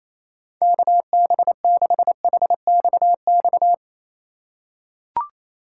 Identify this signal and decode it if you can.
Morse code, heard as a keyed tone, about 28 wpm, 700 Hz. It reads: KB6HXX
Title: USA Call Signs - 6 Characters - 4x Character Spacing - 28wpm